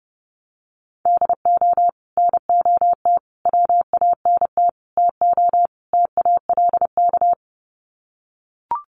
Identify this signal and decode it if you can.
Morse code, heard as a keyed tone, about 30 wpm, 700 Hz. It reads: DO NOT WANT TO TALK
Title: Sets of 5 Words 30wpm